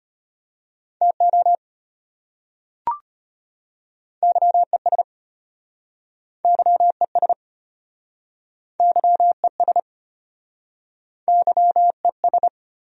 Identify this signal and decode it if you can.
Morse code, heard as a keyed tone, about 33 wpm, 700 Hz. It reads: TO YES YES YES YES